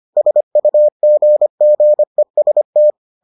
Morse code, heard as a keyed tone, about 25 wpm, 600 Hz. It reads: SUGGEST